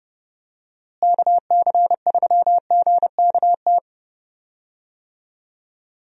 Morse code, heard as a keyed tone, about 30 wpm, 700 Hz. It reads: KC3GKT
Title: USA Call Signs - 6 Characters - Repeated 3x - 30wpm